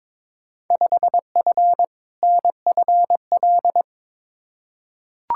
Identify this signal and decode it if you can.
Morse code, heard as a keyed tone, about 22 wpm, 700 Hz. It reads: 5F NFL
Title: Field Day Exchanges Encore - 22wpm